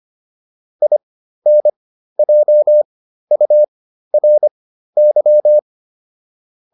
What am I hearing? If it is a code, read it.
Morse code, heard as a keyed tone, about 25 wpm, spaced as for 13 wpm, 600 Hz. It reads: INJURY